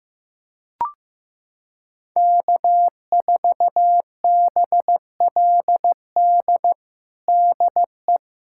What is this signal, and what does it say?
Morse code, heard as a keyed tone, about 15 wpm, 700 Hz. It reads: K4BLD DE
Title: USA Callsign DE Callsign 15wpm